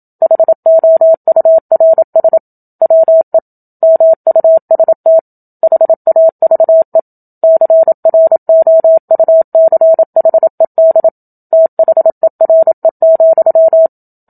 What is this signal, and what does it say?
Morse code, heard as a keyed tone, about 27 wpm, 650 Hz. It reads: HOURS WE MUST HAVE CROUCHED THERE,
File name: Book Two / 01-Chapter 1 - 27 WPM